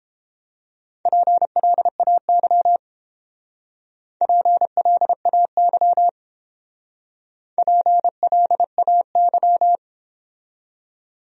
Morse code, heard as a keyed tone, about 29 wpm, 700 Hz. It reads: PLAY PLAY PLAY